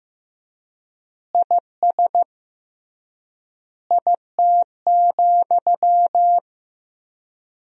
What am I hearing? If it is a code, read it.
Morse code, heard as a keyed tone, about 15 wpm, 700 Hz. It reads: IS IT,
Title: 3x Word-Spacing - Sentences from Top 200 Words - 15wpm